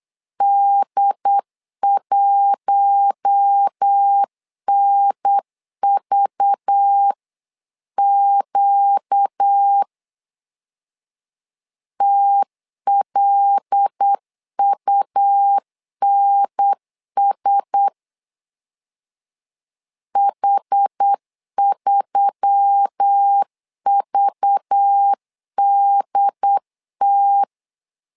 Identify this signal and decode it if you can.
Morse code, heard as a keyed tone, about 8 wpm, 800 Hz. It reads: D1NVQ TLUNS H3VDT
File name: LEZIONI / 5 LBVC / 5 LBV 8wpm 2837